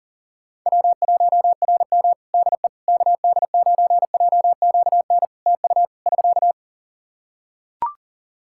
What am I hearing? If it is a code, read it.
Morse code, heard as a keyed tone, about 37 wpm, 700 Hz. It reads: W1RM DE KD9JQN TU <SK>
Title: K1USN SST QSO Practice 40wpm